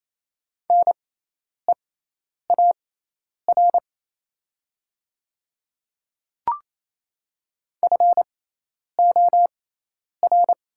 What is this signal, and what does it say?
Morse code, heard as a keyed tone, about 28 wpm, spaced as for 10 wpm, 700 Hz. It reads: NEAR FOR
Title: 6x Character Spacing - Top 200 Words - 28wpm